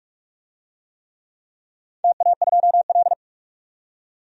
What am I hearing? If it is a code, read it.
Morse code, heard as a keyed tone, about 44 wpm, 700 Hz. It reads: TA2L
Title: DXCC-100 Call Signs - 4 Characters - 45wpm